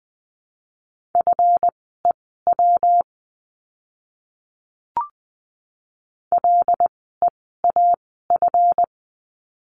Morse code, heard as a keyed tone, about 20 wpm, spaced as for 14 wpm, 700 Hz. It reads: FEW LEAF
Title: 139 Lesson 23f - Letter f - Words 20wpm